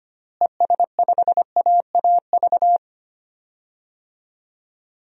Morse code, heard as a keyed tone, about 25 wpm, 700 Hz. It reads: ES5AAV
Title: DXCC-100 Call Signs - 6 Characters - Repeated 3x - 25wpm